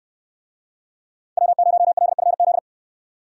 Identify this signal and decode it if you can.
Morse code, heard as a keyed tone, about 69 wpm, 700 Hz. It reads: W0LFZ